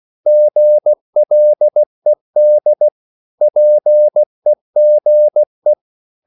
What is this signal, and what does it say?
Morse code, heard as a keyed tone, about 16 wpm, 600 Hz. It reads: GLED PEGE
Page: Korte danske ord 16wpm